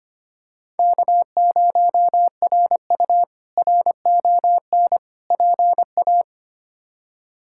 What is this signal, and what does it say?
Morse code, heard as a keyed tone, about 25 wpm, 700 Hz. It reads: K0RU RON PA
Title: K1USN SST QSO Practice - 25wpm